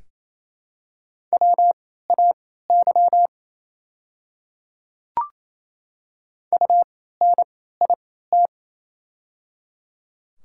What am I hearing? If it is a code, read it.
Morse code, heard as a keyed tone, about 28 wpm, spaced as for 16 wpm, 700 Hz. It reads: WAY UNIT
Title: 3x Character Spacing - Top 500 Words - 28wpm